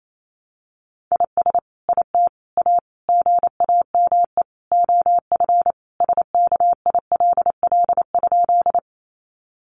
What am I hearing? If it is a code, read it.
Morse code, heard as a keyed tone, about 28 wpm, 700 Hz. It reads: IS IT A GAME OF SKILL?
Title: Sentences from Top 1200 Words 28wpm